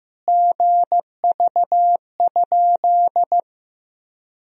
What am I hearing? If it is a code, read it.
Morse code, heard as a keyed tone, about 15 wpm, 700 Hz. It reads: GV?
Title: Sentences from Top 200 Words and Abbreviations 15wpm